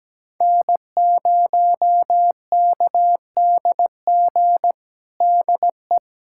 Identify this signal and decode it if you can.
Morse code, heard as a keyed tone, about 17 wpm, 700 Hz. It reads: N0KDG DE
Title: USA Callsign DE Callsign 17wpm